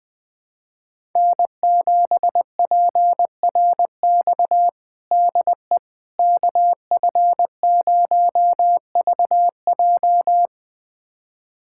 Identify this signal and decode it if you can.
Morse code, heard as a keyed tone, about 20 wpm, 700 Hz. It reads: N7PRX DE KF0VJ
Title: USA Callsign DE Callsign - 5 Characters - 20wpm